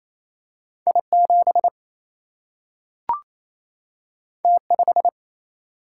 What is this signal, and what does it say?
Morse code, heard as a keyed tone, about 28 wpm, 700 Hz. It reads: I 7 T 5
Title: Single-Letter Single-Number 28wpm